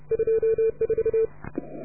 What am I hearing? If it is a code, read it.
Morse code, heard as a keyed tone, about 33 wpm, 440 Hz. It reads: 24